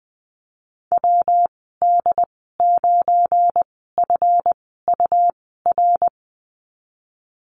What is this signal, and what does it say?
Morse code, heard as a keyed tone, about 20 wpm, spaced as for 15 wpm, 700 Hz. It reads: WD9FUR
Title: 173 Lesson 28g - Number 9 - Call Signs 20wpm